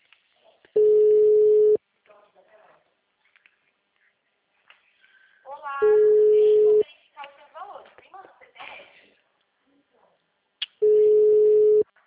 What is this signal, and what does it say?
Ringback tone (Europe, 425 Hz), with speech.
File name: out- -2011- - - .39355